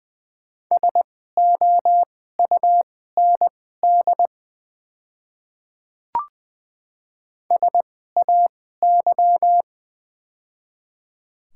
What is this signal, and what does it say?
Morse code, heard as a keyed tone, about 20 wpm, spaced as for 15 wpm, 690 Hz. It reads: SOUND SAY
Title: 2x Character Spacing - Top 200 Words - 20wpm